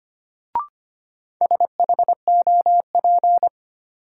Morse code, heard as a keyed tone, about 25 wpm, 700 Hz. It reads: SHOP
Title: Sentences from Top 1600 Words - New Word Drill - 25wpm